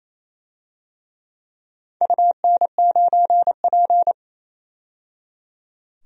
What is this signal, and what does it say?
Morse code, heard as a keyed tone, about 28 wpm, 700 Hz. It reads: UN9P
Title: DXCC-100 Call Signs - 4 Characters - 28wpm